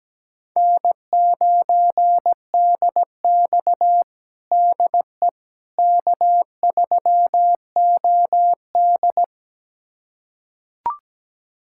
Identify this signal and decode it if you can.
Morse code, heard as a keyed tone, about 17 wpm, 700 Hz. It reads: N9DX DE K3OD
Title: USA Callsign DE Callsign - 4 Characters - 17wpm